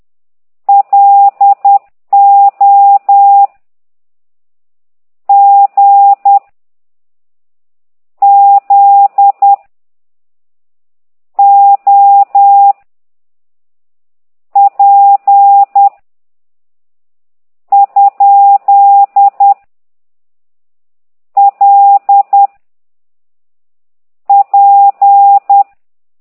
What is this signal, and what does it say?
Morse code, heard as a keyed tone, about 10 wpm, 800 Hz. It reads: L O G Z O P ? L P
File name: ct2idw / exerc13